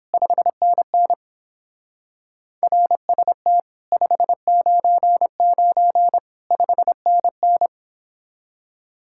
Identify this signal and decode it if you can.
Morse code, heard as a keyed tone, about 29 wpm, 700 Hz. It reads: 5NN RST 599 5NN